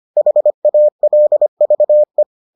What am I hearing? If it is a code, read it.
Morse code, heard as a keyed tone, about 25 wpm, 600 Hz. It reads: HALVE